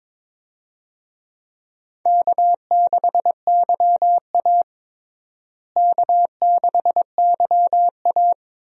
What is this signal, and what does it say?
Morse code, heard as a keyed tone, about 22 wpm, 700 Hz. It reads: K6YA K6YA
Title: USA Call Signs - 4 Characters - Repeated 2x - 22wpm